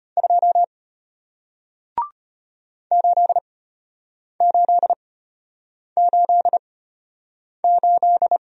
Morse code, heard as a keyed tone, about 33 wpm, spaced as for 8 wpm, 700 Hz. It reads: 2 8888